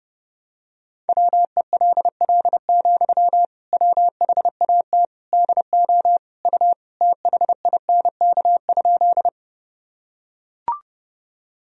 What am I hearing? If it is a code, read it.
Morse code, heard as a keyed tone, about 32 wpm, 700 Hz. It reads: WELL, WHAT DO U THINK?
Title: Sentences from Top 200 Words and Abbreviations 30wpm